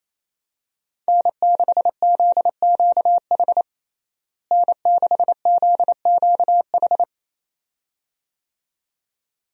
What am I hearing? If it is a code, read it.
Morse code, heard as a keyed tone, about 28 wpm, 700 Hz. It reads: N6ZQH N6ZQH
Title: USA Call Signs - 5 Characters - Repeated 2x - 28wpm